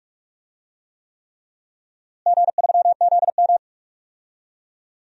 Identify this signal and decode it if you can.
Morse code, heard as a keyed tone, about 46 wpm, 700 Hz. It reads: G3ZM
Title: DXCC-100 Call Signs - 4 Characters - Repeated 3x - 45wpm